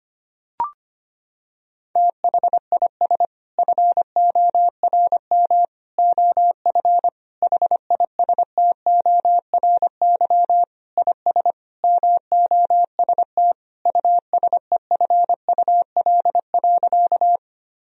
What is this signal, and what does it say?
Morse code, heard as a keyed tone, about 25 wpm, 700 Hz. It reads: THIS FORM OF HISTORY IS MOST USEFUL.
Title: Sentences from Top 1000 Words - Review Part 4 - 25wpm